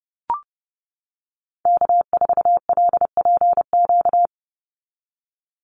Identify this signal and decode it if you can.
Morse code, heard as a keyed tone, about 30 wpm, 700 Hz. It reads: K4LPQ
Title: NA SOTA Activator Callsigns 30wpm